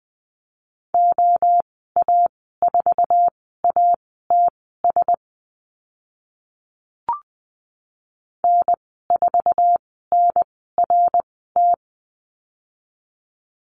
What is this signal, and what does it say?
Morse code, heard as a keyed tone, about 20 wpm, spaced as for 15 wpm, 700 Hz. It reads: OA4ATS N4NRT
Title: 069 Lesson 12g - Letter d - Call Signs 20wpm